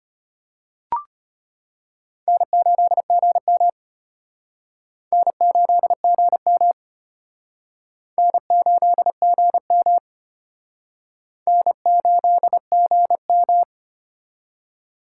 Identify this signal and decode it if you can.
Morse code, heard as a keyed tone, about 31 wpm, 700 Hz. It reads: N8GM N8GM N8GM N8GM
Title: Speed Racing - USA 1x2 Call Signs 25wpm